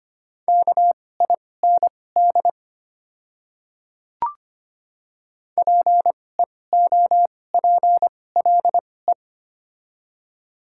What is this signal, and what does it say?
Morse code, heard as a keyed tone, about 25 wpm, spaced as for 18 wpm, 700 Hz. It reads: KIND PEOPLE
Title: 2x Character Spacing - Top 200 Words - 25wpm